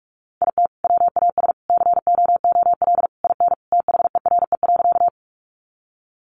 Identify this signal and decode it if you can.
Morse code, heard as a keyed tone, about 44 wpm, 700 Hz. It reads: IT WAS COOL IN THERE.
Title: Sentences from Top 1200 Words 45wpm